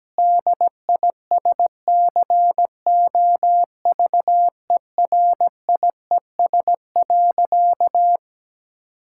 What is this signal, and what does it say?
Morse code, heard as a keyed tone, about 17 wpm, 700 Hz. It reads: DISCOVERIES.
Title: Pangram Challenge v2 - 17wpm